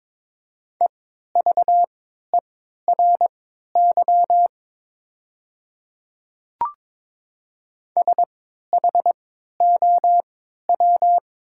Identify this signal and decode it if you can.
Morse code, heard as a keyed tone, about 22 wpm, spaced as for 13 wpm, 700 Hz. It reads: EVERY SHOW
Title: 3x Character Spacing - Top 200 Words - 22wpm